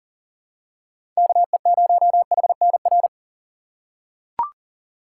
Morse code, heard as a keyed tone, about 38 wpm, 700 Hz. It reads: KE0HNR
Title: USA Call Signs - 6 Characters - 4x Character Spacing - 40wpm